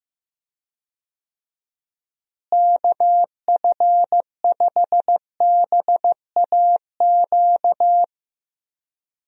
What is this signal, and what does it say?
Morse code, heard as a keyed tone, about 15 wpm, 700 Hz. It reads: KF5BAQ
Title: USA Call Signs - 6 Characters - Repeated 2x - 15wpm